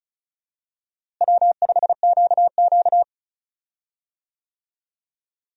Morse code, heard as a keyed tone, about 36 wpm, 700 Hz. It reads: W5QQ
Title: USA Call Signs - 4 Characters - Repeated 3x - 35wpm